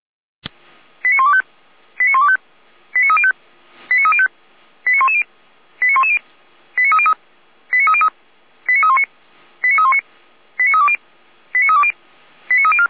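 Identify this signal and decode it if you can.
Five-tone selective call (ZVEI), heard as quick runs of five tones, tones of about 70 ms. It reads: 89125 89125 89305 89305 89110 89110 89303 89303 89319 89319 89320 89320 89304
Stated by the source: ZVEI five-tone selective call, code 89097, received by SDR.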